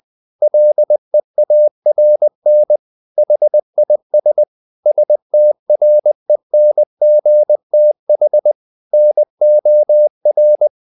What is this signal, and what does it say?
Morse code, heard as a keyed tone, about 20 wpm, 600 Hz. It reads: LEARN HIS STRENGTH NOR
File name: the old man and the sea / the old man and the sea part 1 20wpm